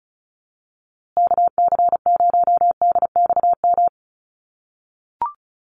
Morse code, heard as a keyed tone, about 34 wpm, 700 Hz. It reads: KC0DXM
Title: USA Call Signs Repeated 2x 35wpm